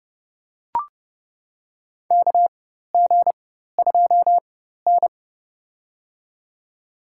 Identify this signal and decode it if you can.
Morse code, heard as a keyed tone, about 32 wpm, spaced as for 14 wpm, 700 Hz. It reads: KG2N
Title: USA Call Signs - 4 Characters - 4x Character Spacing - 30wpm